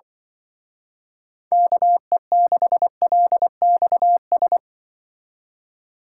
Morse code, heard as a keyed tone, about 24 wpm, 700 Hz. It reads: KE6LXS
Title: Speed Racing - USA 2x3 Call Signs 20wpm